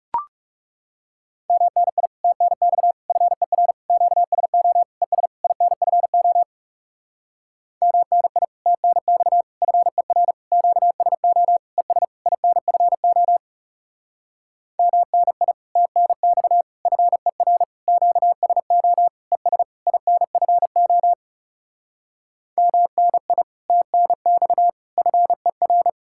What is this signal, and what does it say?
Morse code, heard as a keyed tone, about 40 wpm, 700 Hz. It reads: MNI TNX FER QSO ES INFO MNI TNX FER QSO ES INFO MNI TNX FER QSO ES INFO MNI TNX FER